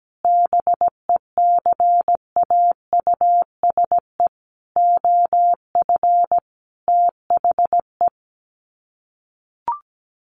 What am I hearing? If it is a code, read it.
Morse code, heard as a keyed tone, about 17 wpm, 700 Hz. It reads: BECAUSE OF THE
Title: Sets of 5 Words Encore2 17wpm